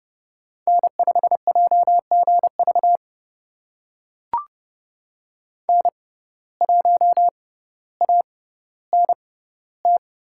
Morse code, heard as a keyed tone, about 29 wpm, spaced as for 15 wpm, 700 Hz. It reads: N5JGV N1ANT
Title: USA Call Signs - 6x Character Spacing - 30wpm